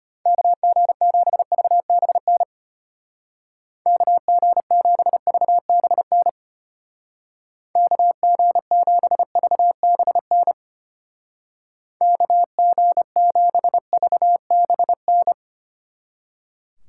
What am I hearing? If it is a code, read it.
Morse code, heard as a keyed tone, about 33 wpm, 700 Hz. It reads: KG7VBN KG7VBN KG7VBN KG7VBN